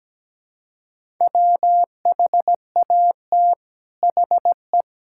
Morse code, heard as a keyed tone, about 17 wpm, 700 Hz. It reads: WHAT HE
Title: Sentences from Top 1000 Words - Review Part 6 - 17wpm